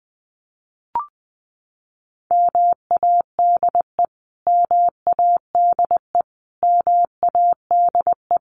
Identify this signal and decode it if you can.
Morse code, heard as a keyed tone, about 20 wpm, 700 Hz. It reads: MADE MADE MADE
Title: Top 300 Words Repeated 20wpm